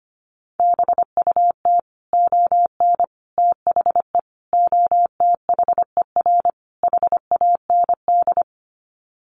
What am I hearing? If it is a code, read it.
Morse code, heard as a keyed tone, about 25 wpm, 700 Hz. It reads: BUT ON THE OTHER HAND
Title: Sets of 5 Words 25wpm